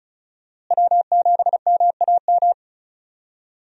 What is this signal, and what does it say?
Morse code, heard as a keyed tone, about 36 wpm, 700 Hz. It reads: W7MAM
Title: USA Call Signs - 5 Characters - Repeated 3x - 35wpm